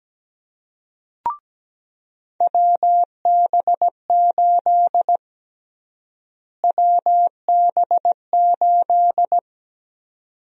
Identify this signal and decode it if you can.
Morse code, heard as a keyed tone, about 17 wpm, 700 Hz. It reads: WB8 WB8
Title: USA Call Sign Prefixes - Repeated 2x - 17wpm